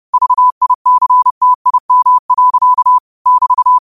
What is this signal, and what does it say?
Morse code, heard as a keyed tone, about 30 wpm, 1000 Hz. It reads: VICTIM. <BT>
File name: 30wpm